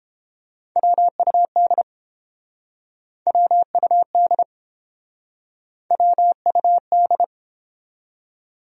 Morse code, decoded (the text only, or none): WUD WUD WUD